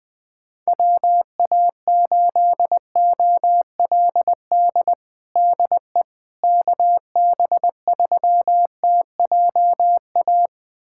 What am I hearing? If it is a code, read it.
Morse code, heard as a keyed tone, about 21 wpm, 700 Hz. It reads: WA8OLD DE KB3TJA
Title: USA Callsign DE Callsign - 6 Characters - 20wpm